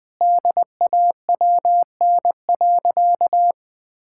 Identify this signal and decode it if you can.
Morse code, heard as a keyed tone, about 20 wpm, 700 Hz. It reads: DAWN.
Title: Difficult Sentences with Hints - 20wpm